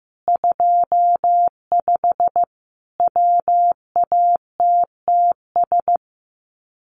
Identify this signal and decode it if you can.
Morse code, heard as a keyed tone, about 15 wpm, 700 Hz. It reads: 25 WATTS
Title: Common QSO Phrases 15wpm